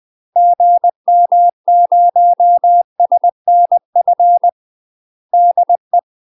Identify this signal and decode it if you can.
Morse code, heard as a keyed tone, about 20 wpm, 700 Hz. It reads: GM0SNF DE
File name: UK MASTER QSOs TRAINING / 20WPM / UK MASTER QSOs 0023